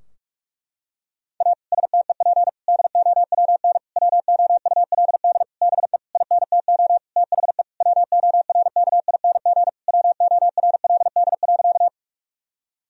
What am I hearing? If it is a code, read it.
Morse code, heard as a keyed tone, about 46 wpm, 700 Hz. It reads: A STEP DOWN WOULD BE INTO THE WORKING WORLD.
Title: Sentences from Top 1000 Words - Review Part 4 - 45wpm